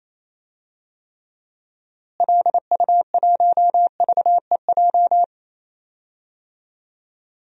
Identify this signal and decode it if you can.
Morse code, heard as a keyed tone, about 28 wpm, 700 Hz. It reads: LU1VEJ
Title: DXCC-100 Call Signs - 6 Characters - Repeated 3x - 28wpm